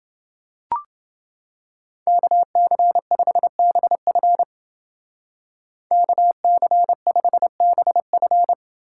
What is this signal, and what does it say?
Morse code, heard as a keyed tone, about 29 wpm, 700 Hz. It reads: KC5BF KC5BF